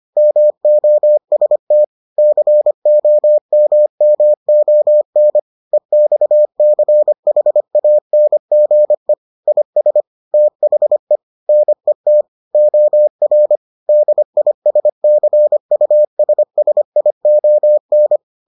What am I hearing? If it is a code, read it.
Morse code, heard as a keyed tone, about 25 wpm, 600 Hz. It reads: MOST COMMON EXCHANGE IS THE NET OR DISCUSSION